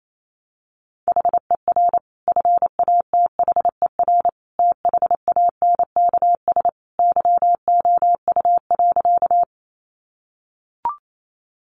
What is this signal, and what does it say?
Morse code, heard as a keyed tone, about 28 wpm, 700 Hz. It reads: HER FATHER THANKS YOU.
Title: Sentences from Top 600 Words 28wpm